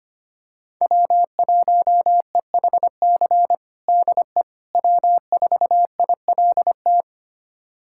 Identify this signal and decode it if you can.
Morse code, heard as a keyed tone, about 25 wpm, 700 Hz. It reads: W1EHC DE W4ILT
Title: USA Callsign DE Callsign - 5 Characters - 25wpm